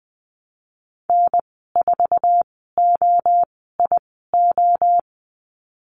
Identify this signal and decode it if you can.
Morse code, heard as a keyed tone, about 20 wpm, spaced as for 15 wpm, 700 Hz. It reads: N4OIO
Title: 048 Lesson 9g - Number 4 - Call Signs 20wpm